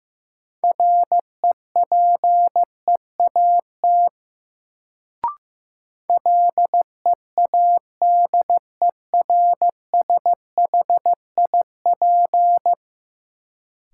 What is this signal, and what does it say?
Morse code, heard as a keyed tone, about 15 wpm, 700 Hz. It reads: REPEAT LEADERSHIP
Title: Sentences from Top 1600 Words - New Word Drill - 15wpm